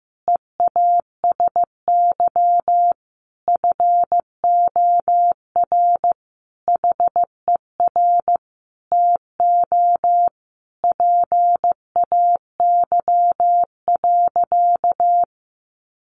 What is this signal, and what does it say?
Morse code, heard as a keyed tone, about 15 wpm, 700 Hz. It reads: EASY FOR HER TO PAY.
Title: Sentences from Top 1500 Words 15wpm